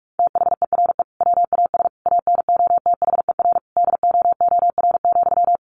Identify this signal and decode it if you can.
Morse code, heard as a keyed tone, about 44 wpm, 700 Hz. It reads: THERE WAS ANOTHER DOOR,
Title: Sentences from Top 1300 Words 45wpm